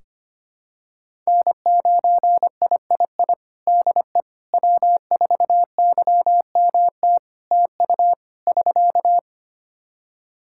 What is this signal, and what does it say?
Morse code, heard as a keyed tone, about 25 wpm, 700 Hz. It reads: N9III DE W4YMT TU <SK>
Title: Practice QSO - Exchange 4 - Ending - 25wpm